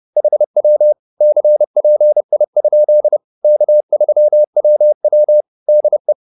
Morse code, heard as a keyed tone, about 30 wpm, 600 Hz. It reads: HW CPI? K3WW DE
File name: QSO101 30